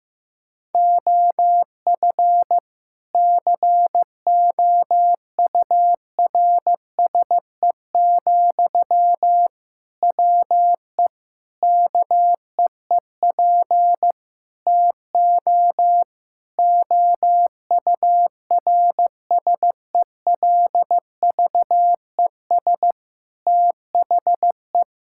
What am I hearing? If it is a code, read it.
Morse code, heard as a keyed tone, about 16 wpm, 700 Hz. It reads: OF COURSE, WE KEEP TO OURSELVES THE
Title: Sentences from Top 1600 Words 15wpm